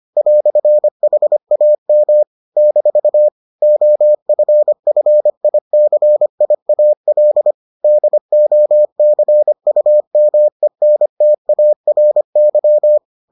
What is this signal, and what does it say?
Morse code, heard as a keyed tone, about 25 wpm, 600 Hz. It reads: "HAM - OFFICIAL DOCUMENTARY